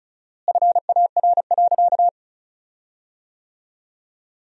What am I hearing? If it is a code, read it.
Morse code, heard as a keyed tone, about 34 wpm, 700 Hz. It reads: FAR.
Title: 4x Word-Spacing - Sentences from Top 400 Words - 35wpm